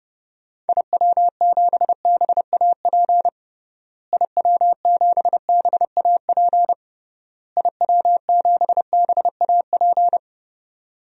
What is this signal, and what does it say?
Morse code, heard as a keyed tone, about 32 wpm, 700 Hz. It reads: IW7BAP IW7BAP IW7BAP
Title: DXCC-100 Call Signs - 6 Characters - Repeated 3x - 30wpm